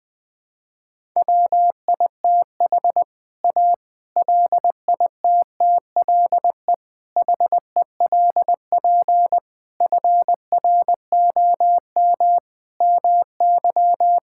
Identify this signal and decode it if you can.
Morse code, heard as a keyed tone, about 20 wpm, 700 Hz. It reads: WITH A LITTLE HELP FROM MY
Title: Sentences with 7-Word Phrases - 20wpm